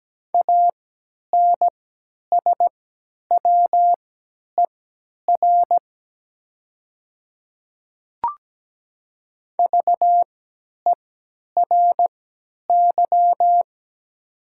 Morse code, heard as a keyed tone, about 17 wpm, spaced as for 10 wpm, 700 Hz. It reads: ANSWER VERY
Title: 3x Character Spacing - Top 500 Words - 17wpm